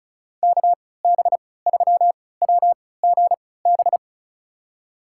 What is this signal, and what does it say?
Morse code, heard as a keyed tone, about 36 wpm, spaced as for 20 wpm, 700 Hz. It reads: KB3WGB
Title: USA Call Signs - 3x Character Spacing - 35wpm